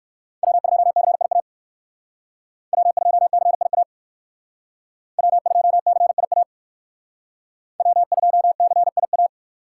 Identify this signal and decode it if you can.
Morse code, heard as a keyed tone, about 57 wpm, 700 Hz. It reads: W2CIA W2CIA W2CIA W2CIA